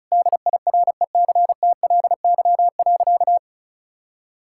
Morse code, heard as a keyed tone, about 36 wpm, 700 Hz. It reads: DIRECTLY.
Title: Sentences from Top 1000 Words - Review Part 5 - 35wpm